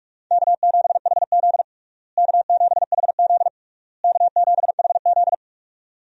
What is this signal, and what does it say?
Morse code, heard as a keyed tone, about 46 wpm, 700 Hz. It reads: K7HZ K7HZ K7HZ
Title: USA Call Signs - 4 Characters - Repeated 3x - 45wpm